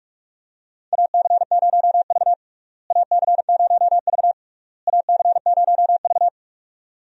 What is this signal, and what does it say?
Morse code, heard as a keyed tone, about 45 wpm, 700 Hz. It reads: AC0V AC0V AC0V
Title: USA Call Signs - 4 Characters - Repeated 3x - 45wpm